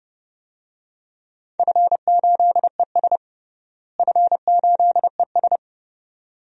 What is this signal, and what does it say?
Morse code, heard as a keyed tone, about 32 wpm, 700 Hz. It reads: F8ES F8ES
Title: DXCC-100 Call Signs - 4 Characters - Repeated 2x - 30wpm